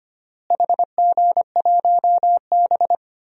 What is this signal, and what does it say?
Morse code, heard as a keyed tone, about 25 wpm, 700 Hz. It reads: HG1B
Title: DXCC-100 Call Signs - 4 Characters - Repeated 3x - 25wpm